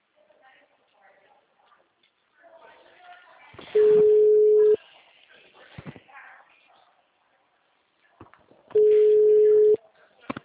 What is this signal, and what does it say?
Ringback tone (Europe, 425 Hz), with speech.